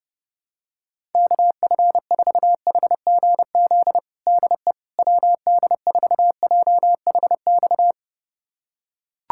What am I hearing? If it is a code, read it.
Morse code, heard as a keyed tone, about 30 wpm, 700 Hz. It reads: KF4HGZ DE WD4JHX
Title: USA Callsign DE Callsign - 6 Characters - 30wpm